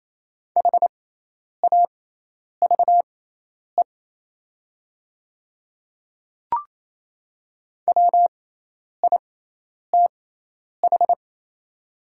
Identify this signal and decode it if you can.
Morse code, heard as a keyed tone, about 28 wpm, spaced as for 10 wpm, 690 Hz. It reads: HAVE WITH
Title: 6x Character Spacing - Top 100 Words - 28wpm